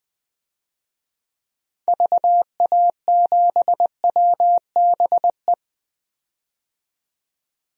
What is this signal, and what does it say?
Morse code, heard as a keyed tone, about 21 wpm, 700 Hz. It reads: VA7WBE
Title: DXCC-100 Call Signs - 6 Characters - 20wpm